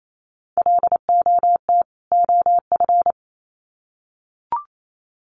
Morse code, heard as a keyed tone, about 28 wpm, 700 Hz. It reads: LOT OF
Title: Sets of 2 Words 28wpm